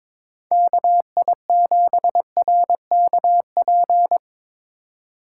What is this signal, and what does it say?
Morse code, heard as a keyed tone, about 25 wpm, 700 Hz. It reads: KI7RKP
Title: Speed Racing - USA 2x3 Call Signs 22wpm